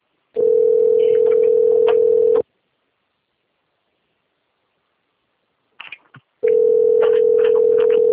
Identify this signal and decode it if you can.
Ringback tone (North America), 440 + 480 Hz.
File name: rg-1600-unknown- - - .4611